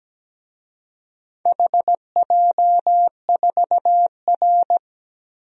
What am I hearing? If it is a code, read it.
Morse code, heard as a keyed tone, about 17 wpm, 700 Hz. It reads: HJ4R
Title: DXCC-100 Call Signs - 4 Characters - 17wpm